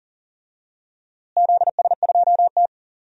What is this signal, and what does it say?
Morse code, heard as a keyed tone, about 40 wpm, 700 Hz. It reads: ZS2T
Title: DXCC-100 Call Signs - 4 Characters - Repeated 3x - 40wpm